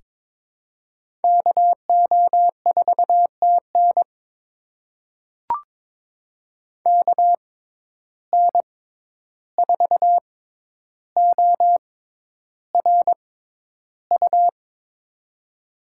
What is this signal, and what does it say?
Morse code, heard as a keyed tone, about 22 wpm, spaced as for 11 wpm, 700 Hz. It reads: KO4TN KN4ORU
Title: USA Call Signs - 6x Character Spacing - 22wpm